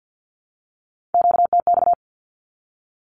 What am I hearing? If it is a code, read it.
Morse code, heard as a keyed tone, about 48 wpm, 700 Hz. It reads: QTX
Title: Q-Codes 50wpm